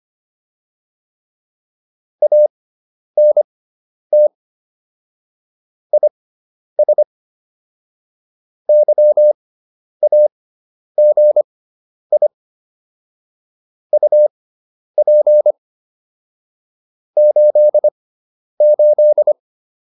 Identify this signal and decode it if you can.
Morse code, heard as a keyed tone, about 25 wpm, spaced as for 10 wpm, 600 Hz. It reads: ANT IS YAGI UP 88
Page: intermediate-practice-files